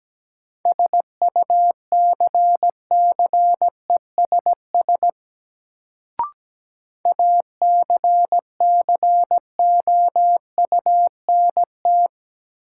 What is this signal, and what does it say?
Morse code, heard as a keyed tone, about 17 wpm, 700 Hz. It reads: SUCCESS ACCOUNT
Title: Sentences from Top 900 Words - New Word Drill - 17wpm